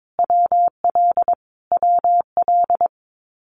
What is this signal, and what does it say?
Morse code, heard as a keyed tone, about 22 wpm, 700 Hz. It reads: WL WL
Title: Abbreviations Repeated 3x - 22wpm